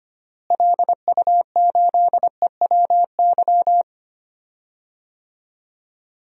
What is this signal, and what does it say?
Morse code, heard as a keyed tone, about 25 wpm, 700 Hz. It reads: LU8EWY
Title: DXCC-100 Call Signs - 6 Characters - Repeated 2x - 25wpm